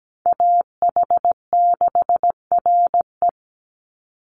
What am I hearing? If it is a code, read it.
Morse code, heard as a keyed tone, about 17 wpm, 700 Hz. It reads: AH6RE
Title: Call Signs Repeated 3x - 17wpm